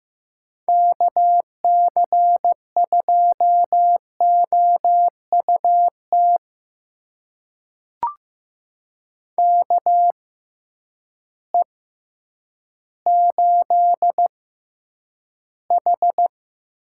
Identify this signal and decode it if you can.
Morse code, heard as a keyed tone, about 15 wpm, spaced as for 9 wpm, 700 Hz. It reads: KC2OUT KE8H
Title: USA Call Signs - 6 Characters - 6x Character Spacing - 15wpm